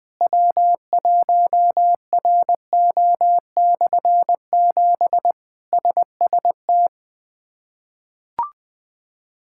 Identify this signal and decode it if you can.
Morse code, heard as a keyed tone, about 21 wpm, 700 Hz. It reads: W1RO/7 SST
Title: K1USN SST QSO Practice - 20wpm